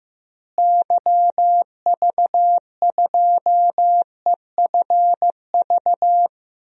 Morse code, heard as a keyed tone, about 16 wpm, 700 Hz. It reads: YV2EFV
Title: DXCC-100 Call Signs - 6 Characters - Repeated 2x - 15wpm